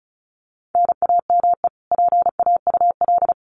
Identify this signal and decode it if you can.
Morse code, heard as a keyed tone, about 34 wpm, 700 Hz. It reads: NAME PAUL
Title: Common QSO Phrases 35wpm